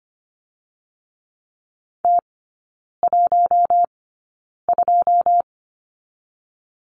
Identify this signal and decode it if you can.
Morse code, heard as a keyed tone, about 25 wpm, spaced as for 9 wpm, 690 Hz. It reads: T12
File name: Lesson 14e - Number 2 - Three Characters 25wpm